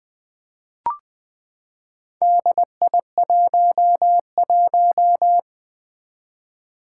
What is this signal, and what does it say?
Morse code, heard as a keyed tone, about 20 wpm, 700 Hz. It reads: DI11
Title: Four Challenging Letter-Number Combinations 20wpm